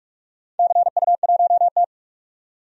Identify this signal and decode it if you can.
Morse code, heard as a keyed tone, about 46 wpm, 700 Hz. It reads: CU1T
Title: DXCC-100 Call Signs - 4 Characters - 45wpm